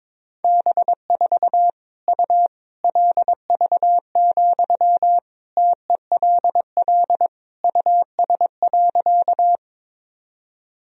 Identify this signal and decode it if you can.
Morse code, heard as a keyed tone, about 22 wpm, 700 Hz. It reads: B4 U LV, TELL US.
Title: Sentences from Top 400 Words and Abbreviations 22wpm